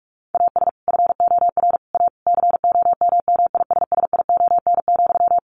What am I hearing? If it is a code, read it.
Morse code, heard as a keyed tone, about 44 wpm, 700 Hz. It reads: AS FOR A COMMISSION,
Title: Sentences from Top 1400 Words 45wpm